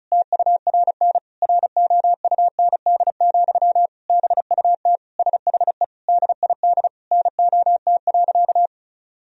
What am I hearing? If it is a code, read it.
Morse code, heard as a keyed tone, about 36 wpm, 700 Hz. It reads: TURN ROUND, BUT SHE DID NOT.
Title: Sentences from Top 1000 Words - Review Part 1 - 35wpm